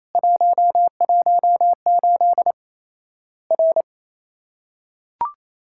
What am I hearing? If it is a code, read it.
Morse code, heard as a keyed tone, about 28 wpm, 700 Hz. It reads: 118 R
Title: Worked All Europe DX Contest - QTC Exchange - 28wpm